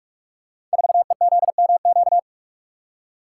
Morse code, heard as a keyed tone, about 44 wpm, 700 Hz. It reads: 4EZMQ
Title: Code Groups - 5 Letter:Numbers - 45wpm